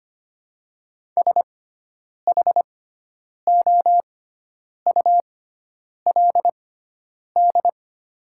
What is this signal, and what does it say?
Morse code, heard as a keyed tone, about 25 wpm, spaced as for 9 wpm, 700 Hz. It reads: SHOULD
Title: 6x Character Spacing - Top 300 Words - 25wpm